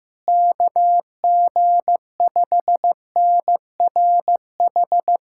Morse code, heard as a keyed tone, about 15 wpm, 700 Hz. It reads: KG5NRH
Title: USA Callsign DE Callsign - 6 Characters - 15wpm